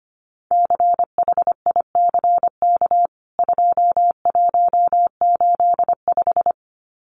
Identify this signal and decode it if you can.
Morse code, heard as a keyed tone, about 25 wpm, 700 Hz. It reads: CHICK 2185
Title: CWT Exchange 25wpm